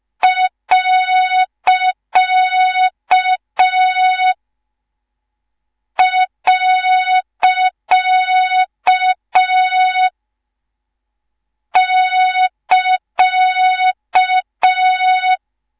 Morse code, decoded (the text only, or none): . <KA>